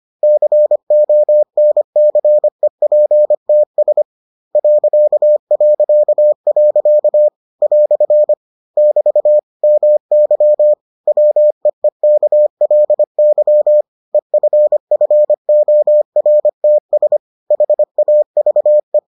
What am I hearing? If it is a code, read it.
Morse code, heard as a keyed tone, about 25 wpm, 600 Hz. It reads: CONCEPTS ... " <BT> MY WEEKLY EFFORTS HAVE